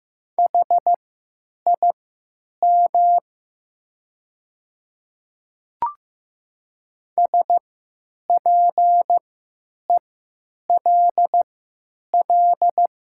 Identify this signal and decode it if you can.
Morse code, heard as a keyed tone, about 15 wpm, spaced as for 9 wpm, 700 Hz. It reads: HIM SPELL
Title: 3x Character Spacing - Top 300 Words - 15wpm